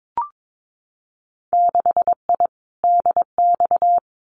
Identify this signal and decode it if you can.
Morse code, heard as a keyed tone, about 22 wpm, 700 Hz. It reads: 6I DX
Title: Winter Field Day Exchanges Encore 22wpm